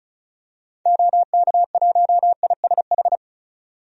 Morse code, heard as a keyed tone, about 36 wpm, 690 Hz. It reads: OK1ISH
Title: DXCC-100 Call Signs - 6 Characters - 35wpm